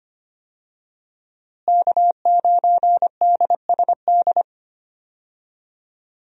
Morse code, heard as a keyed tone, about 25 wpm, 700 Hz. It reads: K9DSD
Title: USA Call Signs - 5 Characters - Repeated 3x - 25wpm